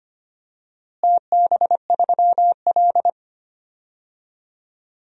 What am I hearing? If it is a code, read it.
Morse code, heard as a keyed tone, about 25 wpm, 700 Hz. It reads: TB3L
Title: DXCC-100 Call Signs - 4 Characters - Repeated 3x - 25wpm